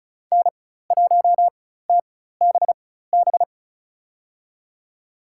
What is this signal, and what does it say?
Morse code, heard as a keyed tone, about 36 wpm, spaced as for 16 wpm, 700 Hz. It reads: N1TBB
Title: USA Call Signs - 5 Characters - 4x Character Spacing - 35wpm